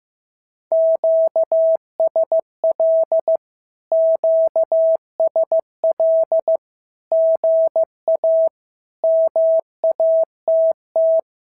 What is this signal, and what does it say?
Morse code, heard as a keyed tone, about 15 wpm, 650 Hz. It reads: QSL QSL GA MATT
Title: SOTA QSO Practice 15wpm